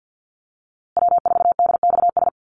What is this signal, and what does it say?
Morse code, heard as a keyed tone, about 48 wpm, 700 Hz. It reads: W4DXS
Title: USA Call Signs Repeated 2x 50wpm